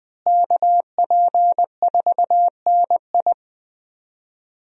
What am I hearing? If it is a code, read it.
Morse code, heard as a keyed tone, about 20 wpm, 700 Hz. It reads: KP4NI
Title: USA Call Signs - 5 Characters - Repeated 3x - 20wpm